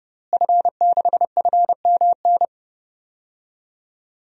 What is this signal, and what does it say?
Morse code, heard as a keyed tone, about 30 wpm, 700 Hz. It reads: F6FMN
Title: DXCC-100 Call Signs - 5 Characters - Repeated 3x - 30wpm